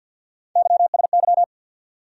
Morse code, heard as a keyed tone, about 50 wpm, 700 Hz. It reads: YSY
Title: USA Call Sign Postfixes - 3 Letters 50wpm